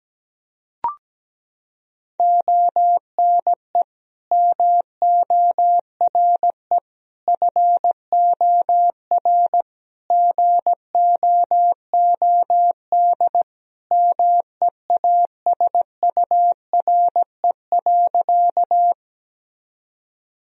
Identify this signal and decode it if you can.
Morse code, heard as a keyed tone, about 17 wpm, 700 Hz. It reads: ONE MORE FOR GOOD MEASURE.
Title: Sentences from Top 1000 Words - Review Part 2 - 17wpm